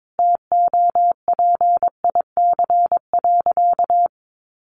Morse code, heard as a keyed tone, about 22 wpm, 700 Hz. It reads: TOPIC.
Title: Sentences from Top 1200 Words 22wpm